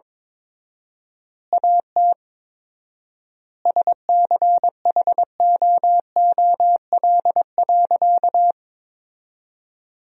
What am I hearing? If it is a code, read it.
Morse code, heard as a keyed tone, about 22 wpm, 700 Hz. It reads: AT SCHOOL.
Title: 4x Word-Spacing - Sentences from Top 300 Words - 22wpm